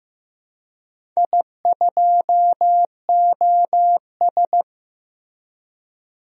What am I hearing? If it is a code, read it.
Morse code, heard as a keyed tone, about 15 wpm, 700 Hz. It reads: I2OS
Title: DXCC-100 Call Signs - 4 Characters - Repeated 3x - 15wpm